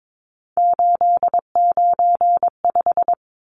Morse code, heard as a keyed tone, about 22 wpm, 700 Hz. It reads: 895
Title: Numbers - 3-Digit 22wpm